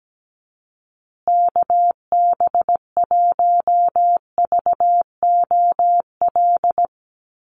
Morse code, heard as a keyed tone, about 17 wpm, 700 Hz. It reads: KB1VOL
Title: USA Call Signs Repeated 3x 17wpm